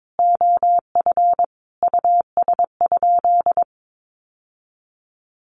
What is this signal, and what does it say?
Morse code, heard as a keyed tone, about 22 wpm, 700 Hz. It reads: OF US?
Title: Sentences from Top 1400 Words 22wpm